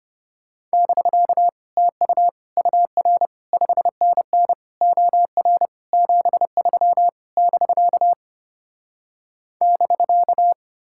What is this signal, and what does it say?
Morse code, heard as a keyed tone, about 33 wpm, 700 Hz. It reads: <BK> TU UR 5NN OR 73 <BK> <BK>